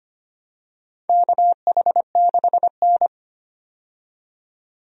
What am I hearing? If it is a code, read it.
Morse code, heard as a keyed tone, about 25 wpm, 700 Hz. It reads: KH6N
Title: DXCC-100 Call Signs - 4 Characters - 25wpm